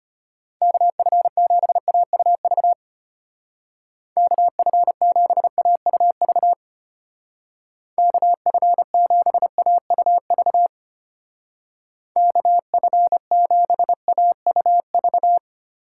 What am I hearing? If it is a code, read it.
Morse code, heard as a keyed tone, about 32 wpm, 700 Hz. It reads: KF7AUV KF7AUV KF7AUV KF7AUV